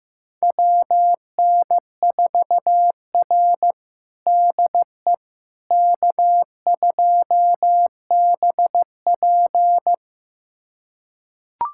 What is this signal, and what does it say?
Morse code, heard as a keyed tone, about 16 wpm, 700 Hz. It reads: WN4R DE K2BP
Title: USA Callsign DE Callsign - 4 Characters - 15wpm